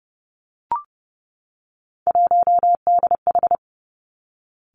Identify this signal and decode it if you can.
Morse code, heard as a keyed tone, about 30 wpm, 700 Hz. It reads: 1DH
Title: Three Challenging Letter-Number Combinations 30wpm